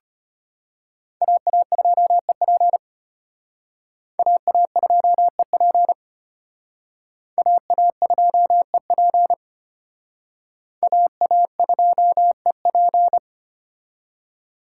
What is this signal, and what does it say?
Morse code, heard as a keyed tone, about 33 wpm, 700 Hz. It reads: AA2EP AA2EP AA2EP AA2EP